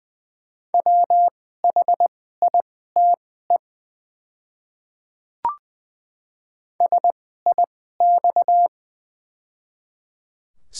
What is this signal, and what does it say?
Morse code, heard as a keyed tone, about 20 wpm, spaced as for 15 wpm, 700 Hz. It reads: WHITE SIX
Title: 2x Character Spacing - Top 500 Words - 20wpm